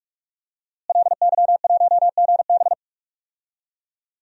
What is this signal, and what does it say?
Morse code, heard as a keyed tone, about 41 wpm, 700 Hz. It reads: LY1GB
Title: DXCC-100 Call Signs - 5 Characters - Repeated 3x - 45wpm